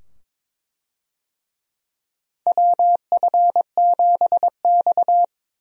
Morse code, heard as a keyed tone, about 22 wpm, 700 Hz. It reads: WF7X
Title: USA Call Signs - 4 Characters - 22wpm